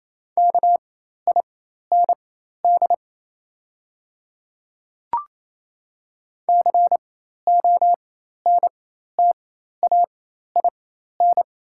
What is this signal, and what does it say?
Morse code, heard as a keyed tone, about 28 wpm, spaced as for 13 wpm, 700 Hz. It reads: KIND CONTAIN
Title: 4x Character Spacing - Top 500 Words - 28wpm